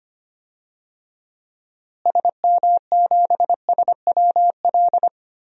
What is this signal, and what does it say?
Morse code, heard as a keyed tone, about 25 wpm, 700 Hz. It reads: SM7SWL
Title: DXCC-100 Call Signs - 6 Characters - Repeated 2x - 25wpm